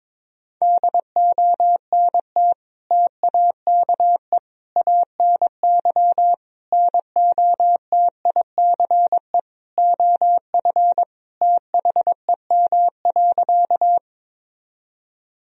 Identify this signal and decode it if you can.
Morse code, heard as a keyed tone, about 22 wpm, 700 Hz. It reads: DONT TAKE ANY NOTICE OF THEM.
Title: Sentences from Top 1000 Words - Review Part 2 - 22wpm